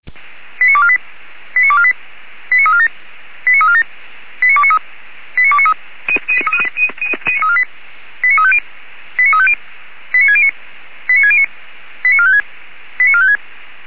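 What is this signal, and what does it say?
Five-tone selective call (ZVEI), heard as quick runs of five tones, tones of about 70 ms. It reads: 89248 89248 89347 89347 89293 89293 89350 89350 89609 89609 89456 89456
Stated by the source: ZVEI five-tone call received by SDR.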